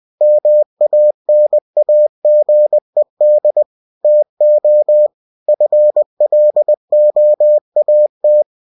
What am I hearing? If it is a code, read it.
Morse code, heard as a keyed tone, about 20 wpm, 600 Hz. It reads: MANAGED TO FLOAT
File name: cw-books / winnie-the-pooh / winnie-20wpm-page-0104